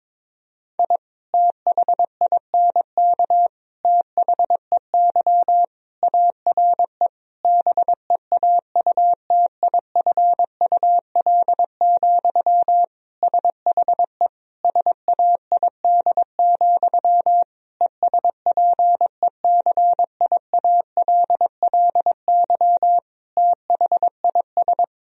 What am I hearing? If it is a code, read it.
Morse code, heard as a keyed tone, about 22 wpm, 700 Hz. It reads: I THINK THEY ARE BEAUTIFUL, SHE SAID, ESPECIALLY THIS
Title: Sentences from Top 1000 Words - Review Part 4 - 22wpm